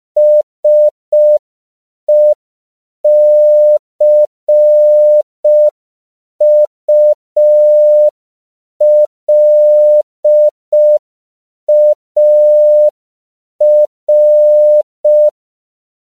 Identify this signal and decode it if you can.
Morse code, heard as a keyed tone, about 5 wpm, 600 Hz. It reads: SECULAR
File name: CW / 5PPM